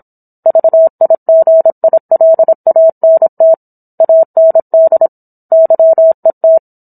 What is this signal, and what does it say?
Morse code, heard as a keyed tone, about 26 wpm, 650 Hz. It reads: VIGILANT AND YET